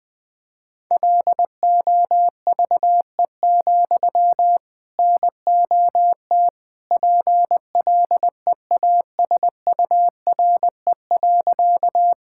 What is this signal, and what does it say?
Morse code, heard as a keyed tone, about 21 wpm, 700 Hz. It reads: LOVE, NOT PLEASURE.
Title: Sentences from Top 1600 Words 20wpm